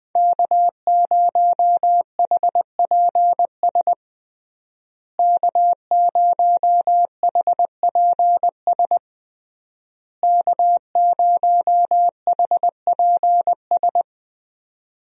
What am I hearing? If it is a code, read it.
Morse code, heard as a keyed tone, about 21 wpm, 700 Hz. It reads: K0HPS K0HPS K0HPS
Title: USA Call Signs - 5 Characters - Repeated 3x - 20wpm